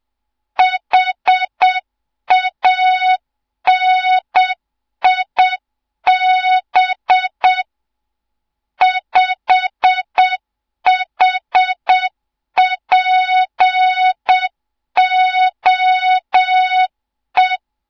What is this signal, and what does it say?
Morse code, heard as a keyed tone, about 7 wpm, 3700 Hz. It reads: HANIB 5HPOE
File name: Morze-HA2MN / hang / 07 GYAKORLAT